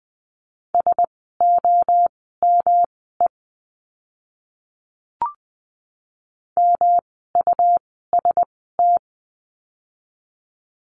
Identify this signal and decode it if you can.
Morse code, heard as a keyed tone, about 20 wpm, spaced as for 15 wpm, 700 Hz. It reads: SOME MUST
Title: 237 Lesson 38f - Letter q - Words 20wpm